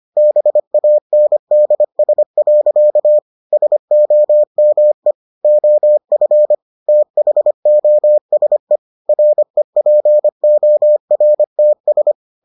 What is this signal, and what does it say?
Morse code, heard as a keyed tone, about 25 wpm, 600 Hz. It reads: BANDS. SOME OF THOSE REPORTS